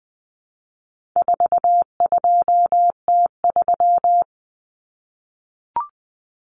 Morse code, heard as a keyed tone, about 20 wpm, 700 Hz. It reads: 42T3
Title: Common Cut Numbers - 4-Digit 20wpm